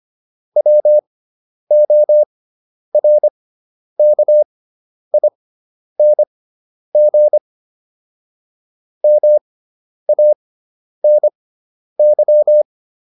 Morse code, heard as a keyed tone, about 25 wpm, spaced as for 10 wpm, 600 Hz. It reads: WORKING MANY